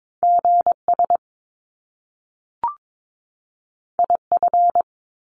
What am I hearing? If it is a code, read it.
Morse code, heard as a keyed tone, about 22 wpm, 700 Hz. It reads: GS IF
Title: Top Two Letter Combinations in Words 22wpm